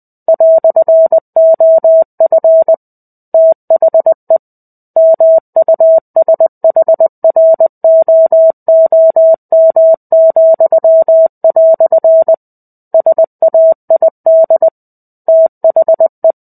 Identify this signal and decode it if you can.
Morse code, heard as a keyed tone, about 20 wpm, 650 Hz. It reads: "OF THE MUSHROOM," SAID THE
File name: 05-Chapter-05 20 WPM